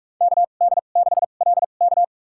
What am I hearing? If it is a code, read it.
Morse code, heard as a keyed tone, about 46 wpm, spaced as for 32 wpm, 700 Hz. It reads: KD6LK
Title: USA Call Signs - 5 Characters - 2x Character Spacing - 45wpm